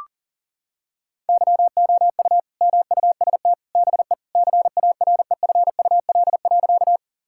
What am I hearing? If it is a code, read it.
Morse code, heard as a keyed tone, about 40 wpm, 700 Hz. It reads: YOU MUST BE CAREFUL.
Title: Sentences from Top 1600 Words 40wpm